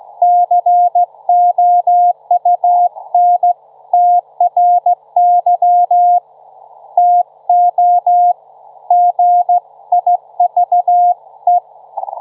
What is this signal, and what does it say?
Morse code, heard as a keyed tone, about 15 wpm, 710 Hz. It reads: COUNTRY TO GIVE
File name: Trivia-ZL2WT